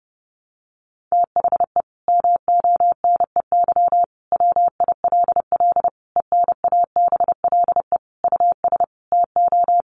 Morse code, heard as a keyed tone, about 30 wpm, 700 Hz. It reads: THE MONEY WILL ENABLE US TO
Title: Sentences from Top 1500 Words 30wpm